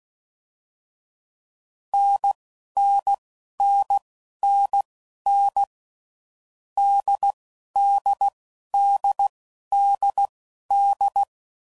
Morse code, heard as a keyed tone, about 16 wpm, spaced as for 11 wpm, 800 Hz. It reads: NNNNN DDDDD
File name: Mixed / Short Runs / NDB NDB6